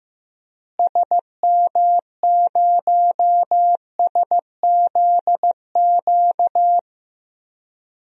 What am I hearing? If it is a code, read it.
Morse code, heard as a keyed tone, about 15 wpm, 700 Hz. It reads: SM0SZQ
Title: DXCC-100 Call Signs - 6 Characters - Repeated 3x - 15wpm